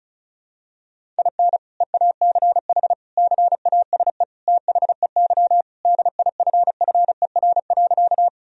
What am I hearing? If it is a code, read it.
Morse code, heard as a keyed tone, about 34 wpm, 700 Hz. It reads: IN EACH CASE THEY DIFFER.
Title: Sentences from Top 1000 Words - Review Part 1 - 35wpm